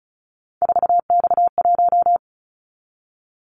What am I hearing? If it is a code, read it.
Morse code, heard as a keyed tone, about 36 wpm, 700 Hz. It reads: 4X1
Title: Callsign Prefixes for Top 50 DXCC - 35wpm